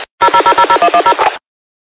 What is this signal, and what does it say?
DTMF, heard as telephone touch tones, tones of about 80 ms. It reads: 00000110
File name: qsorec SimplexLogic